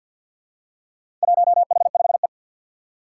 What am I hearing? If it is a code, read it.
Morse code, heard as a keyed tone, about 52 wpm, 700 Hz. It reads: 1H5E
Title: Four Challenging Letter-Number Combinations 50wpm